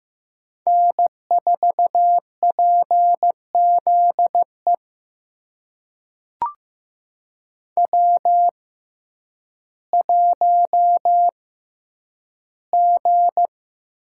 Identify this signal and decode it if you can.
Morse code, heard as a keyed tone, about 15 wpm, spaced as for 9 wpm, 700 Hz. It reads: N4PZE W1G
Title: USA Call Signs - 5 Characters - 6x Character Spacing - 15wpm